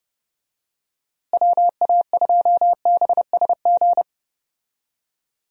Morse code, heard as a keyed tone, about 30 wpm, 700 Hz. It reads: WA2BSG
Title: USA Call Signs - 6 Characters - Repeated 2x - 30wpm